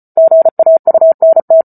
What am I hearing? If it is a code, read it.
Morse code, heard as a keyed tone, about 34 wpm, 650 Hz. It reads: GAUNT